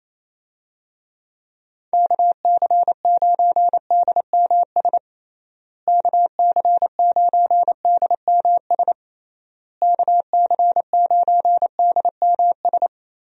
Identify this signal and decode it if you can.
Morse code, heard as a keyed tone, about 27 wpm, 700 Hz. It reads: KC9DMS KC9DMS KC9DMS
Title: USA Call Signs - 6 Characters - Repeated 3x - 28wpm